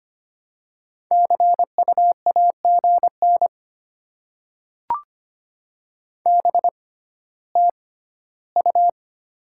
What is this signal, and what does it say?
Morse code, heard as a keyed tone, about 25 wpm, spaced as for 14 wpm, 700 Hz. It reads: CU AGN BTU
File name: Top 20 QSO Elements - 6x Character Spacing - 25wpm